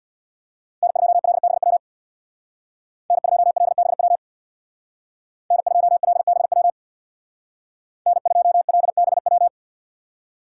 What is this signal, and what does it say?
Morse code, heard as a keyed tone, about 60 wpm, 700 Hz. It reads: N2LBW N2LBW N2LBW N2LBW
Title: Speed Racing - USA 1x3 Call Signs 50wpm